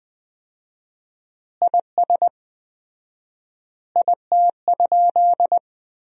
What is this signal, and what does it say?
Morse code, heard as a keyed tone, about 20 wpm, 700 Hz. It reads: IS IT?
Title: 4x Word-Spacing - Sentences from Top 500 Words - 20wpm